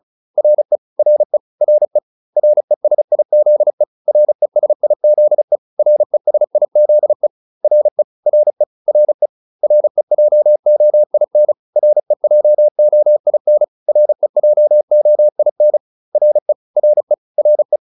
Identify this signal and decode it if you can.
Morse code, heard as a keyed tone, about 34 wpm, 600 Hz. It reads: RE RE RE RESIZE RESIZE RESIZE RE RE RE REJOIN REJOIN REJOIN RE RE RE
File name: RE6 35